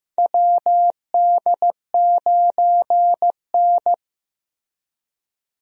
Morse code, heard as a keyed tone, about 15 wpm, 700 Hz. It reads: WD9N
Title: USA Call Signs - 4 Characters - Repeated 3x - 15wpm